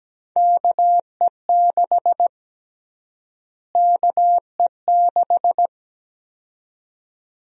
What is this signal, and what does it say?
Morse code, heard as a keyed tone, about 17 wpm, 700 Hz. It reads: KE6 KE6
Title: USA Call Sign Prefixes - Repeated 2x - 17wpm